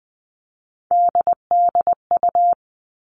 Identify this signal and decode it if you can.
Morse code, heard as a keyed tone, about 20 wpm, 700 Hz. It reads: DDU
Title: All Cut Numbers - 3-Digit 20wpm